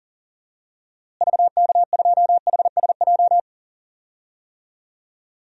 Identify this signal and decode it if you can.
Morse code, heard as a keyed tone, about 40 wpm, 700 Hz. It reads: VK2HSJ
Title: DXCC-100 Call Signs - 6 Characters - Repeated 3x - 40wpm